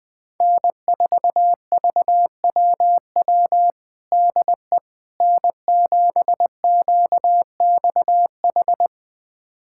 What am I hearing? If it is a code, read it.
Morse code, heard as a keyed tone, about 20 wpm, 700 Hz. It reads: N4VWW DE N7QXH
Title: USA Callsign DE Callsign - 5 Characters - 20wpm